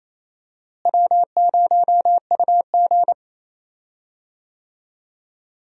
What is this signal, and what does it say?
Morse code, heard as a keyed tone, about 28 wpm, 700 Hz. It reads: W0UG
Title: USA Call Signs - 4 Characters - Repeated 3x - 28wpm